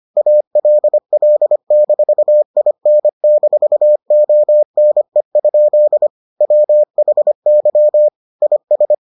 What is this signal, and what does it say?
Morse code, heard as a keyed tone, about 25 wpm, 600 Hz. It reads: ALL-IN-ONE? WHY IS